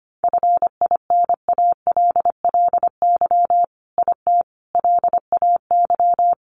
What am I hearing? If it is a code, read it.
Morse code, heard as a keyed tone, about 25 wpm, 700 Hz. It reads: FINALLY IT LAY
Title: Sentences from Top 1000 Words 25wpm